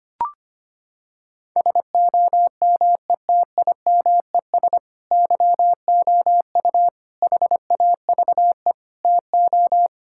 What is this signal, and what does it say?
Morse code, heard as a keyed tone, about 25 wpm, 700 Hz. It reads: SOMETIMES YOU HAVE TO
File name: practice / 25wpm / Sentences with 3-Word Phrases - 25wpm